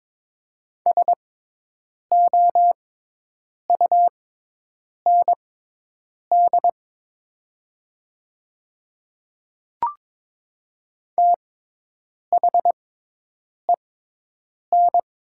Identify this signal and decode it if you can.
Morse code, heard as a keyed tone, about 21 wpm, spaced as for 8 wpm, 700 Hz. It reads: SOUND THEN
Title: 6x Character Spacing - Top 200 Words - 22wpm